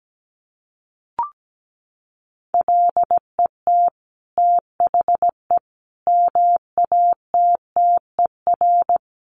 Morse code, heard as a keyed tone, about 17 wpm, 700 Hz. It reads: LET THE MATTER
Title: Sentences from Top 1400 Words 17wpm